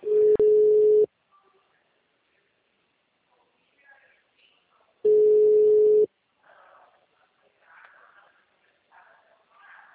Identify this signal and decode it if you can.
Ringback tone (Europe, 425 Hz), with speech.